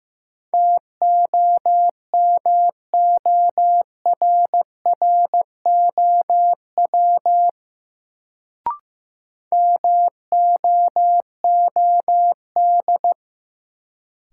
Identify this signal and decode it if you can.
Morse code, heard as a keyed tone, about 15 wpm, 700 Hz. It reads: TOMORROW MOOD
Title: Sentences from Top 1100 Words - New Word Drill - 15wpm